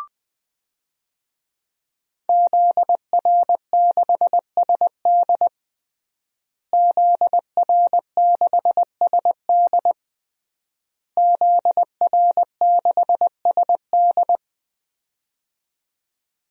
Morse code, heard as a keyed tone, about 20 wpm, 700 Hz. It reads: ZR6SD ZR6SD ZR6SD
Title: DXCC-100 Call Signs - 5 Characters - Repeated 3x - 20wpm